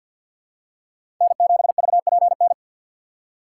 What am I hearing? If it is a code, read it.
Morse code, heard as a keyed tone, about 52 wpm, 700 Hz. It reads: N7VPN
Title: USA Call Signs - 6x Character Spacing - 50wpm